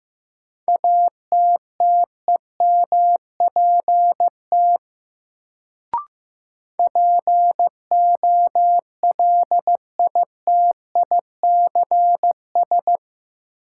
Morse code, heard as a keyed tone, about 15 wpm, 700 Hz. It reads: ATTEMPT POLITICS
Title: Sentences from Top 1000 Words - New Word Drill - 15wpm